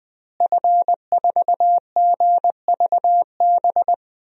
Morse code, heard as a keyed tone, about 21 wpm, 700 Hz. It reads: F4GVB
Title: DXCC-100 Call Signs - 5 Characters - Repeated 3x - 20wpm